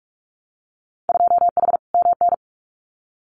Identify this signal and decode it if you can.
Morse code, heard as a keyed tone, about 46 wpm, 700 Hz. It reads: 2H MN
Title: Winter Field Day Exchanges 45wpm